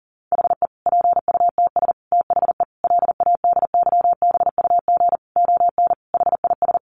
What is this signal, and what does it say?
Morse code, heard as a keyed tone, about 40 wpm, 700 Hz. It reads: HE PUTS THE LADYBUG ON HIS
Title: Additional Easy Phrases 40wpm